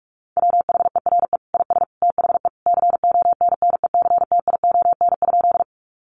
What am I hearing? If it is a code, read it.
Morse code, heard as a keyed tone, about 44 wpm, 700 Hz. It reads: WHERE IS THE CONNECTION?
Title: Sentences from Top 1300 Words 45wpm